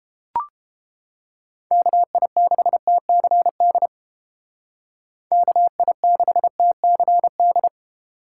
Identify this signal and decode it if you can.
Morse code, heard as a keyed tone, about 31 wpm, 700 Hz. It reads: KI6TCD KI6TCD